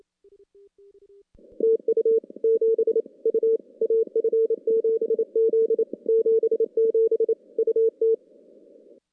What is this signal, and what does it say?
Morse code, heard as a keyed tone, about 28 wpm, 410 Hz. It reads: TU 7 U AF7Z 77 UT
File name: contests / CW-SS-k5tr- -2300